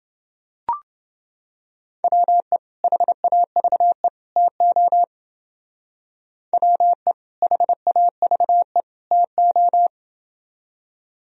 Morse code, heard as a keyed tone, about 28 wpm, 700 Hz. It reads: WE HAVE TO WE HAVE TO